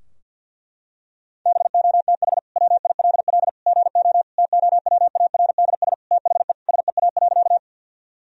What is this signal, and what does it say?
Morse code, heard as a keyed tone, about 48 wpm, 700 Hz. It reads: BOTH WILL GO TOWARDS THE SEA.
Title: Sentences from Top 1000 Words - Review Part 2 - 50wpm